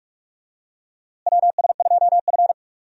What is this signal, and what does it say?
Morse code, heard as a keyed tone, about 46 wpm, 700 Hz. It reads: WS2F
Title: USA Call Signs - 4 Characters - Repeated 3x - 45wpm